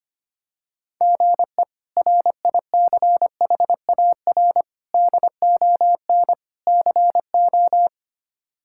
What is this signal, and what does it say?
Morse code, heard as a keyed tone, about 25 wpm, 700 Hz. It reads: GE RICHAR DON CO
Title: K1USN SST QSO Practice - 25wpm